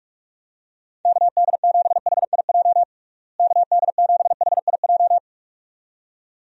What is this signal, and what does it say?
Morse code, heard as a keyed tone, about 46 wpm, 700 Hz. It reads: KD7HIJ KD7HIJ
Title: USA Call Signs - 6 Characters - Repeated 2x - 45wpm